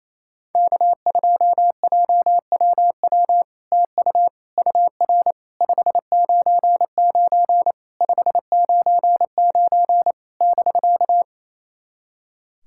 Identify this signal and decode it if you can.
Morse code, heard as a keyed tone, about 29 wpm, 700 Hz. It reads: K2JWW TU UR 599 599 <BK>
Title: SOTA QSO Practice 28wpm